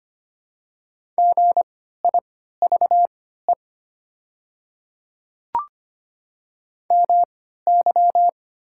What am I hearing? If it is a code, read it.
Morse code, heard as a keyed tone, about 25 wpm, spaced as for 14 wpm, 700 Hz. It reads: GIVE MY
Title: 3x Character Spacing - Top 200 Words - 25wpm